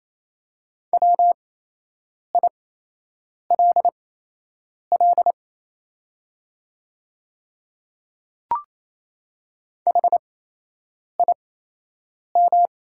Morse code, heard as a keyed tone, about 28 wpm, spaced as for 8 wpm, 700 Hz. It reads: WILL HIM
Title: 8x Character Spacing - Top 100 Words - 28wpm